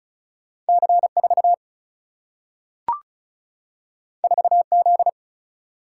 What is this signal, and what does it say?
Morse code, heard as a keyed tone, about 36 wpm, 700 Hz. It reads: C 4 4 Z
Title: Single-Letter Single-Number 35wpm